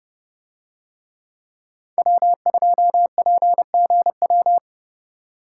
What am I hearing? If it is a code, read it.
Morse code, heard as a keyed tone, about 29 wpm, 700 Hz. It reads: W2PGW